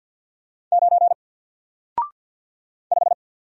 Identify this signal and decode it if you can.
Morse code, heard as a keyed tone, about 49 wpm, 700 Hz. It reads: 9 5
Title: Single Character (Letter:Number:Punctuation:Prosign) Alt 50wpm